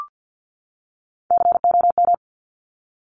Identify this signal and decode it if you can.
Morse code, heard as a keyed tone, about 50 wpm, 700 Hz. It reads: COM
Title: Top Three Letter Combinations in Words 50wpm